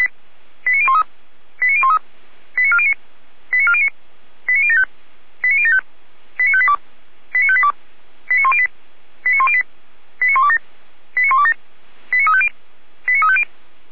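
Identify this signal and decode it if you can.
Five-tone selective call (ZVEI), heard as quick runs of five tones, tones of about 70 ms. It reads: 89013 89013 89409 89409 89075 89075 89582 89582 89108 89108 89127 89127 89350 89350
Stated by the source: SDR receiver, ZVEI five-tone selective call, address 89370.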